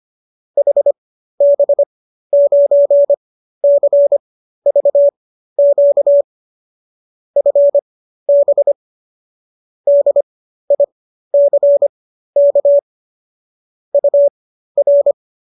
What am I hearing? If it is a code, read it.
Morse code, heard as a keyed tone, about 25 wpm, spaced as for 13 wpm, 570 Hz. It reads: HB9CVQ FB DICK UR